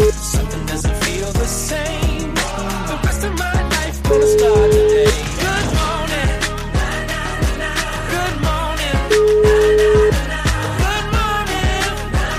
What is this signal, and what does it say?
Ringback tone (Europe, 425 Hz), with noise.